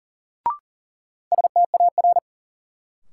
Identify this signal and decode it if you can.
Morse code, heard as a keyed tone, about 40 wpm, 700 Hz. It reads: STAR
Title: Sentences from Top 1000 Words - Review Part 2 - New Word Drill - 40wpm